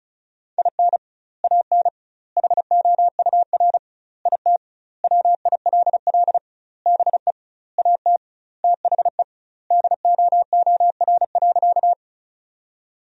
Morse code, heard as a keyed tone, about 36 wpm, 700 Hz. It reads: IN AN HOUR IT WILL BE AT THE DOOR.
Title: 2x Word-Spacing - Sentences from Top 400 Words - 35wpm